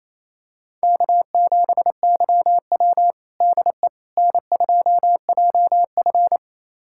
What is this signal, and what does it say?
Morse code, heard as a keyed tone, about 28 wpm, 700 Hz. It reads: K7YW DE N2JF
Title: USA Callsign DE Callsign - 4 Characters - 28wpm